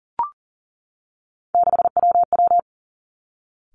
Morse code, heard as a keyed tone, about 40 wpm, 700 Hz. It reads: 6WW
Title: Three Challenging Letter-Number Combinations 40wpm